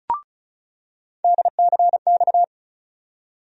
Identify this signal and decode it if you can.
Morse code, heard as a keyed tone, about 34 wpm, 700 Hz. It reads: DCX
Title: USA Call Sign Postfixes - 3 Letters 35wpm